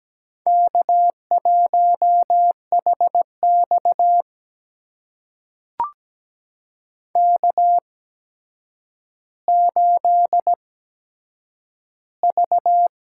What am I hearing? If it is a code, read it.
Morse code, heard as a keyed tone, about 17 wpm, spaced as for 8 wpm, 700 Hz. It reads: K1HX K8V
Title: USA Call Signs - 4 Characters - 8x Character Spacing - 17wpm